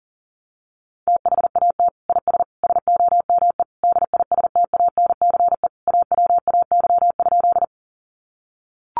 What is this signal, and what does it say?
Morse code, heard as a keyed tone, about 40 wpm, 700 Hz. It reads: THAT IS SOME DISTANCE AWAY?
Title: Sentences from Top 1200 Words 40wpm